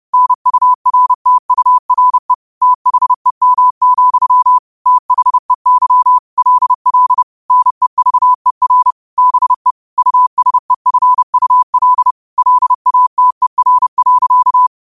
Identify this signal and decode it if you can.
Morse code, heard as a keyed tone, about 30 wpm, 1000 Hz. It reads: NURTURE THEM, THEY LL NEVER BE USEFUL LATER.
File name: qotdfiles / 30wpm